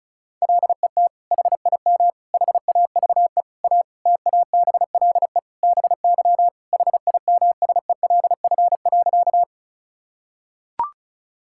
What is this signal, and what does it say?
Morse code, heard as a keyed tone, about 34 wpm, 700 Hz. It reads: LET HIM HAVE A TABLE BY HIMSELF.
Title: Sentences from Top 1600 Words 35wpm